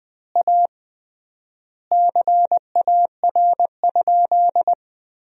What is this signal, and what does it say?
Morse code, heard as a keyed tone, about 20 wpm, 700 Hz. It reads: A CAR?
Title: 3x Word-Spacing - Sentences from Top 300 Words - 20wpm